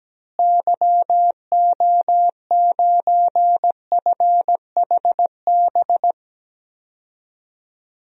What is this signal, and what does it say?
Morse code, heard as a keyed tone, about 17 wpm, 700 Hz. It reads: YO9FHB
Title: DXCC-100 Call Signs - 6 Characters - Repeated 2x - 17wpm